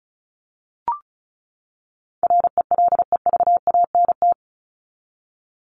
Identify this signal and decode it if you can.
Morse code, heard as a keyed tone, about 34 wpm, 700 Hz. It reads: RELEVANT
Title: Top 1001-1250 Words 35wpm